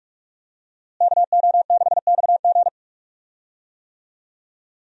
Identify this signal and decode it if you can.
Morse code, heard as a keyed tone, about 46 wpm, 700 Hz. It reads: KO6XG
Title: USA Call Signs - 5 Characters - Repeated 3x - 45wpm